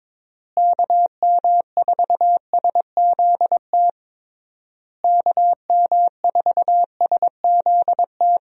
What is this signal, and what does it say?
Morse code, heard as a keyed tone, about 22 wpm, 700 Hz. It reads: KM4SZT KM4SZT
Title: USA Call Signs - 6 Characters - Repeated 3x - 22wpm